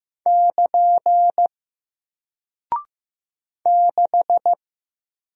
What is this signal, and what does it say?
Morse code, heard as a keyed tone, about 15 wpm, 700 Hz. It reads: <KN> 6
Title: Single Character (Letter:Number:Punctuation:Prosign) Alt 15wpm